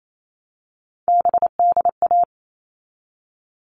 Morse code, heard as a keyed tone, about 28 wpm, 700 Hz. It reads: BDA
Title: All Cut Numbers - 3-Digit 28wpm